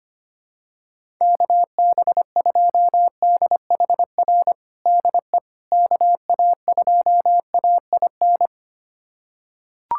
Morse code, heard as a keyed tone, about 25 wpm, 700 Hz. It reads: KB2DHR DE KA2AIN
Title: USA Callsign DE Callsign - 6 Characters - 25wpm